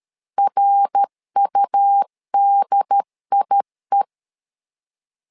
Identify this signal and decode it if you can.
Morse code, heard as a keyed tone, about 13 wpm, 800 Hz. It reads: RUDIE
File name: LEZIONI / 3 MWRO / 3 MWR 12wpm 1711